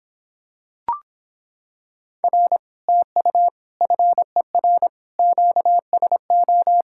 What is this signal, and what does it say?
Morse code, heard as a keyed tone, about 26 wpm, 700 Hz. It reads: R TU FER QSO